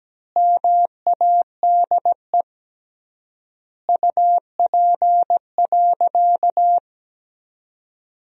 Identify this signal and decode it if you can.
Morse code, heard as a keyed tone, about 17 wpm, 700 Hz. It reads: MADE UP.
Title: 3x Word-Spacing - Sentences from Top 500 Words - 17wpm